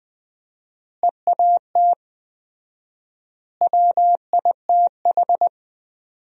Morse code, heard as a keyed tone, about 20 wpm, 700 Hz. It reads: EAT WITH
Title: 4x Word-Spacing - Sentences from Top 300 Words - 20wpm